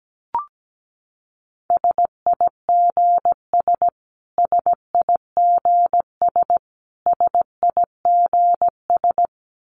Morse code, heard as a keyed tone, about 17 wpm, 700 Hz. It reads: SIGS SIGS SIGS
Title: Abbreviations Repeated 3x - 17wpm